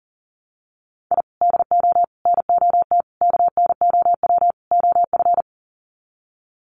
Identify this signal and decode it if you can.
Morse code, heard as a keyed tone, about 40 wpm, 700 Hz. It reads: I DO NOT KNOW OF
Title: Sets of 5 Words Encore2 40wpm